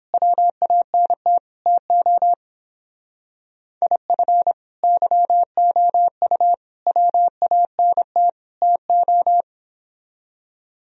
Morse code, heard as a keyed tone, about 26 wpm, 700 Hz. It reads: WANT TO IF YOU WANT TO
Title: Speed-Racing - Sets of 4 Words 22wpm